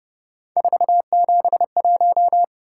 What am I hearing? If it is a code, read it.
Morse code, heard as a keyed tone, about 31 wpm, 700 Hz. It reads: 471